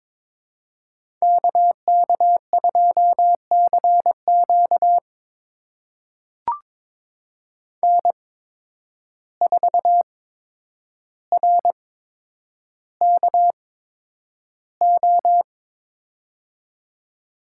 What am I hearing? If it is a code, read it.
Morse code, heard as a keyed tone, about 23 wpm, spaced as for 10 wpm, 700 Hz. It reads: KK2CQ N4RKO
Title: USA Call Signs - 5 Characters - 8x Character Spacing - 22wpm